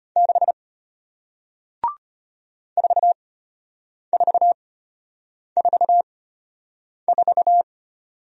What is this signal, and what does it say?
Morse code, heard as a keyed tone, about 34 wpm, spaced as for 8 wpm, 700 Hz. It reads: 6 4444